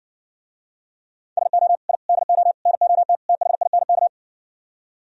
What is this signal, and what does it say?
Morse code, heard as a keyed tone, about 60 wpm, 700 Hz. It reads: SO I DO NOT THINK